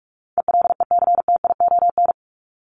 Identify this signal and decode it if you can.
Morse code, heard as a keyed tone, about 44 wpm, 700 Hz. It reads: ELECTION
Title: Top 1251-1500 Words 45wpm